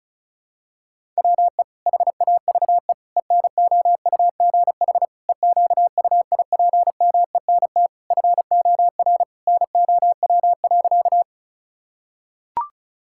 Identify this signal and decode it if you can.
Morse code, heard as a keyed tone, about 36 wpm, 700 Hz. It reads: WE HAVE ENOUGH EQUIPMENT FOR NOW.
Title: Sentences from Top 1000 Words - Review Part 5 - 35wpm